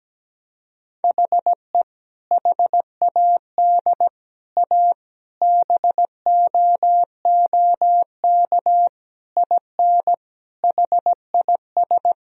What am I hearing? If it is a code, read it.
Morse code, heard as a keyed tone, about 17 wpm, 700 Hz. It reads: HE HAD A BOOK IN HIS
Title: Sentences from Top 1000 Words - Review Part 3 - 17wpm